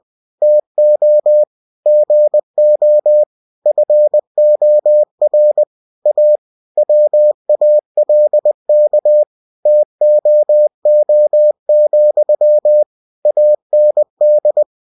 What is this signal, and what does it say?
Morse code, heard as a keyed tone, about 20 wpm, 600 Hz. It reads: TO GO FOR A WALK TOO, AND
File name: cw-books / winnie-the-pooh / winnie-20wpm-page-0016